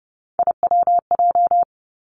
Morse code, heard as a keyed tone, about 30 wpm, 700 Hz. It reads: IWJ
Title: Three Challenging Letter-Number Combinations 30wpm